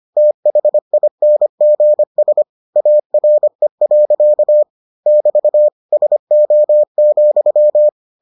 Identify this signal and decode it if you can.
Morse code, heard as a keyed tone, about 25 wpm, 600 Hz. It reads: THINGS ARE. <BT> SO,